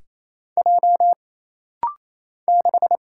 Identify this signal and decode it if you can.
Morse code, heard as a keyed tone, about 27 wpm, 700 Hz. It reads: J 6
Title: Single Letter-Number - Rapid-Fire 28wpm